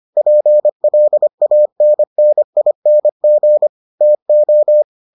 Morse code, heard as a keyed tone, about 25 wpm, 600 Hz. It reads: PLANNING TO